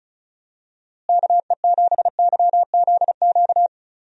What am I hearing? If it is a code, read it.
Morse code, heard as a keyed tone, about 36 wpm, 700 Hz. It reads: KE7YZQ
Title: USA Call Signs - 6 Characters - Repeated 2x - 35wpm